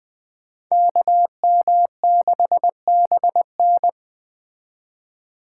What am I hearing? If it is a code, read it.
Morse code, heard as a keyed tone, about 21 wpm, 700 Hz. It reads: KM6BN
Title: Speed Racing - USA 2x2 Call Signs 17wpm